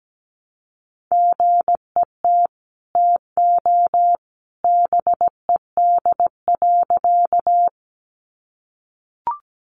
Morse code, heard as a keyed tone, about 17 wpm, 700 Hz. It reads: GET TO BED.
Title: Sentences from Top 500 Words 17wpm